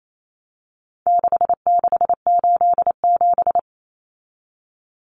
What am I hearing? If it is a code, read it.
Morse code, heard as a keyed tone, about 28 wpm, 700 Hz. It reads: 6687
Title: Common Cut Numbers - 4-Digit 28wpm